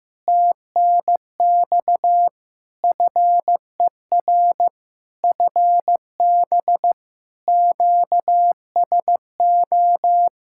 Morse code, heard as a keyed tone, about 15 wpm, 700 Hz. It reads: TNX FER FB QSO
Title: Practice QSO - Exchange 4 - Ending - 15wpm